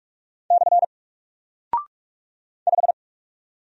Morse code, heard as a keyed tone, about 44 wpm, 700 Hz. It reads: / 5
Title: Single Character (Letter:Number:Punctuation:Prosign) Alt 45wpm